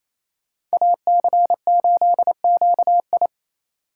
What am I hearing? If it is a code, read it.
Morse code, heard as a keyed tone, about 35 wpm, 700 Hz. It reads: AC8QI